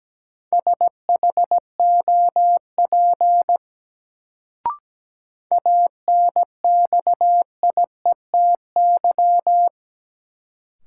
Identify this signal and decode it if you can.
Morse code, heard as a keyed tone, about 17 wpm, 700 Hz. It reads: SHOP ANXIETY
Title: Sentences from Top 1600 Words - New Word Drill - 17wpm